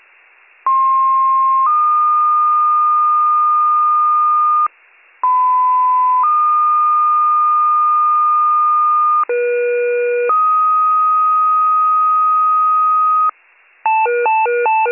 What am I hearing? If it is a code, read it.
Two-tone page, heard as a one-second tone, then a three-second tone, about 1060 then 1220 Hz, about 1010 then 1220 Hz, about 480 then 1250 Hz. It is heard from 0.7 to 13.3 s.